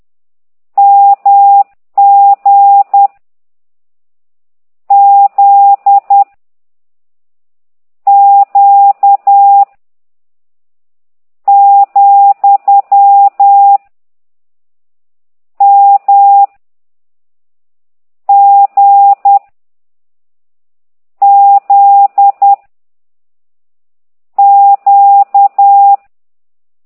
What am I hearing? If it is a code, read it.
Morse code, heard as a keyed tone, about 10 wpm, 800 Hz. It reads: M G Z Q , M G Z Q
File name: ct2idw / exerc13